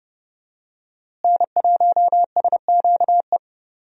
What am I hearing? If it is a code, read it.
Morse code, heard as a keyed tone, about 30 wpm, 700 Hz. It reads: N1SQE
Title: USA Call Signs - 5 Characters - Repeated 2x - 30wpm